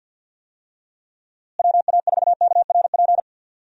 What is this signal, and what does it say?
Morse code, heard as a keyed tone, about 50 wpm, 700 Hz. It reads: WA4KRP
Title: USA Call Signs - 6 Characters - Repeated 3x - 50wpm